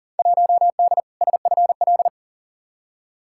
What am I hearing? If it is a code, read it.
Morse code, heard as a keyed tone, about 40 wpm, 700 Hz. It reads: 1D SFL
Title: Field Day Exchanges - 40wpm